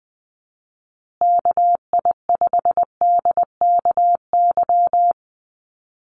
Morse code, heard as a keyed tone, about 20 wpm, 700 Hz. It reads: KI5DKY
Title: USA Call Signs Repeated 2x 20wpm